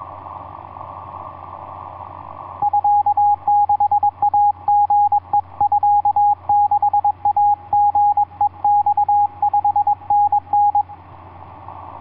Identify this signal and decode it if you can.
Morse code (Japanese Wabun code), heard as a keyed tone, about 22 wpm, 820 Hz, with radio noise. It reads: ミ6イリヘミ6イリヘマ5タタ
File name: kh2l hb9azz etc 12m cw 20mar2010 1023